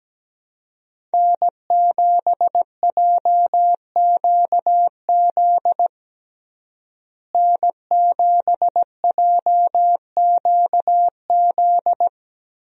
Morse code, heard as a keyed tone, about 17 wpm, 700 Hz. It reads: N7JQZ N7JQZ
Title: USA Call Signs - 5 Characters - Repeated 3x - 17wpm